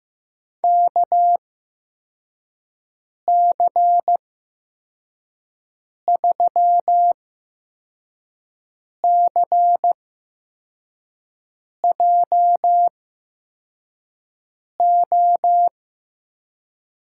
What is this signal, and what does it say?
Morse code, heard as a keyed tone, about 15 wpm, spaced as for 4 wpm, 700 Hz. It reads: KC3CJO
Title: USA Call Signs - 6 Characters - 8x Character Spacing - 15wpm